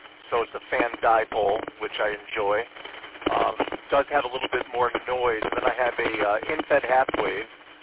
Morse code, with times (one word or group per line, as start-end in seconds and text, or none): none